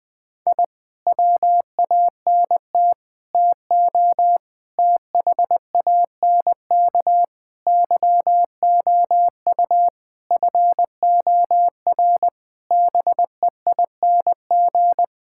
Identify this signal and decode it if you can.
Morse code, heard as a keyed tone, about 20 wpm, 700 Hz. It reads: I WANT TO THANK YOU FOR BEING
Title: Sentences with 9-Word Phrases - Spoken Beginning - 20wpm